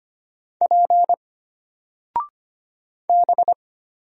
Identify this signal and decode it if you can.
Morse code, heard as a keyed tone, about 25 wpm, 700 Hz. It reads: P B
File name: Single-Letter-Number-25wpm